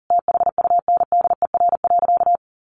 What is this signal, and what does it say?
Morse code, heard as a keyed tone, about 40 wpm, 700 Hz. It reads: THUNDER.
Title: A Sound Of Thunder 40wpm